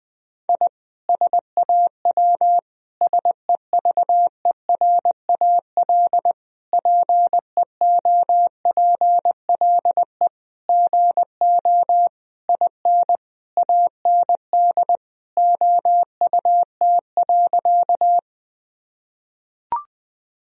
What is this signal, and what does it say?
Morse code, heard as a keyed tone, about 21 wpm, 700 Hz. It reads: I SAW SEVERAL PEOPLE GO IN AND OUT.
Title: Sentences from Top 1000 Words - Review Part 2 - 20wpm